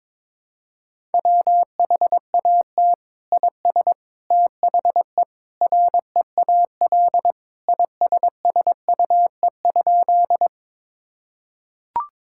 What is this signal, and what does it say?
Morse code, heard as a keyed tone, about 22 wpm, 700 Hz. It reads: WHAT IS THE REAL ISSUE?
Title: Sentences from Top 1000 Words - Review Part 6 - 22wpm